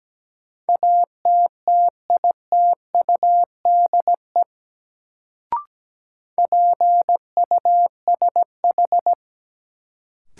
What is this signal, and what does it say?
Morse code, heard as a keyed tone, about 17 wpm, 700 Hz. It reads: ATTITUDE PUSH
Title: Sentences from Top 1300 Words - New Word Drill - 17wpm